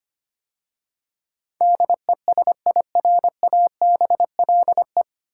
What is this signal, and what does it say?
Morse code, heard as a keyed tone, about 25 wpm, 690 Hz. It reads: DESIRABLE
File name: practice / 25wpm / 9 Letter Words - First 3 Letters Given - 25wpm